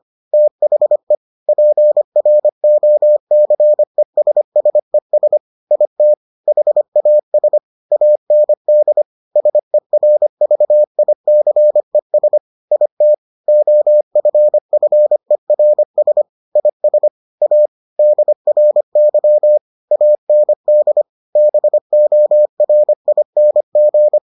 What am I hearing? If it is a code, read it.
Morse code, heard as a keyed tone, about 25 wpm, 600 Hz. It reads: THE PROCESSES IT HAS AND SERVICES IT OFFERS IS A DRY AND BORING